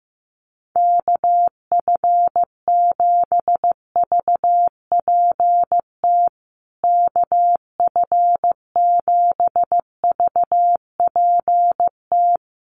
Morse code, heard as a keyed tone, about 15 wpm, 700 Hz. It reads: KF7VPT KF7VPT
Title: USA Call Signs Repeated 3x 15wpm